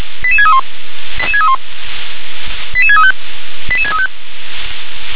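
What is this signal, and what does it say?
Five-tone selective call (ZVEI), heard as quick runs of five tones, tones of about 70 ms. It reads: 88631 88631 88635 88635
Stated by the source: ZVEI five-tone selective call received by SDR.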